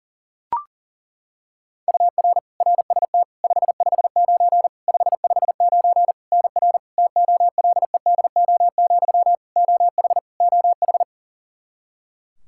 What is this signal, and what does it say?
Morse code, heard as a keyed tone, about 40 wpm, 700 Hz. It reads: UR RST 559 559 NR TOLEDO, OH OH
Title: Practice QSO - Exchange 1 - Beginning - 40wpm